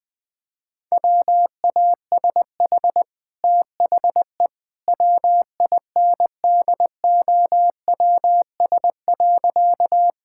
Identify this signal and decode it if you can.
Morse code, heard as a keyed tone, about 20 wpm, 700 Hz. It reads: WASH THE WINDOWS.
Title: Easy Sentences with Hints - 20wpm